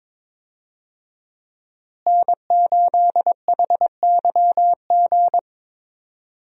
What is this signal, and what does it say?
Morse code, heard as a keyed tone, about 22 wpm, 700 Hz. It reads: N8HYG
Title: USA Call Signs - 5 Characters - Repeated 2x - 22wpm